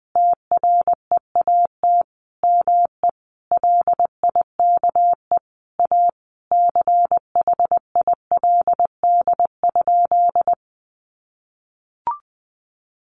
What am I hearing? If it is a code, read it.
Morse code, heard as a keyed tone, about 20 wpm, 700 Hz. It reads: TREAT ME LIKE A CHILD?
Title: Sentences from Top 1300 Words 20wpm